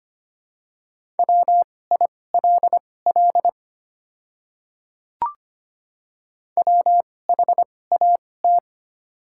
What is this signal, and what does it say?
Morse code, heard as a keyed tone, about 25 wpm, spaced as for 18 wpm, 690 Hz. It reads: WILL WHAT
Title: 2x Character Spacing - Top 500 Words - 25wpm